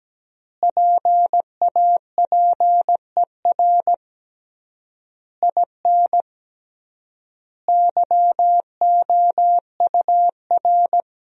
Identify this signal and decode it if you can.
Morse code, heard as a keyed tone, about 17 wpm, 700 Hz. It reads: PAPER IN YOUR
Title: 3x Word-Spacing - Sentences from Top 400 Words - 17wpm